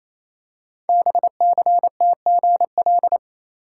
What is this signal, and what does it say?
Morse code, heard as a keyed tone, about 29 wpm, 700 Hz. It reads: BCTGL
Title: Code Groups - 5 Letters - 28wpm